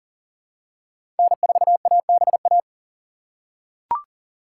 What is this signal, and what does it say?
Morse code, heard as a keyed tone, about 40 wpm, 690 Hz. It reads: N4ABA
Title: USA Call Signs - 5 Characters - 3x Character Spacing - 40wpm